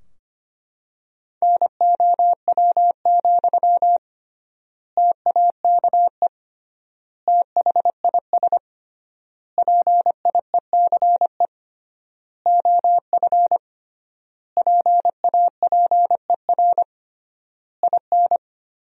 Morse code, heard as a keyed tone, about 25 wpm, 700 Hz. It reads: NOW, TAKE THIS PIECE OF PAPER IN
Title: 3x Word-Spacing - Sentences from Top 400 Words - 25wpm